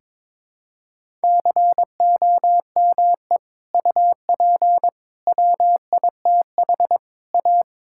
Morse code, heard as a keyed tone, about 22 wpm, 700 Hz. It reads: COME UP WITH A
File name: practice / 22wpm / Sentences with 4-Word Phrases - 22wpm